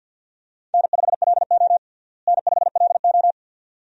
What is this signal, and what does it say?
Morse code, heard as a keyed tone, about 52 wpm, 700 Hz. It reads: N5LO N5LO
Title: USA Call Signs - 4 Characters - Repeated 2x - 50wpm